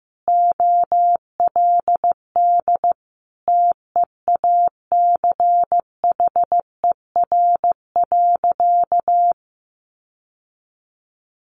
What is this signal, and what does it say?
Morse code, heard as a keyed tone, about 15 wpm, 700 Hz. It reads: OLD TEACHER.
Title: Sentences from Top 900 Words 15wpm